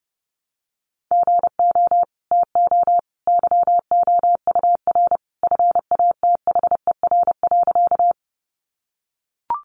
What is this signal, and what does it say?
Morse code, heard as a keyed tone, about 30 wpm, 700 Hz. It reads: GO TO YOUR FATHER.
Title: Sentences from Top 300 Words 30wpm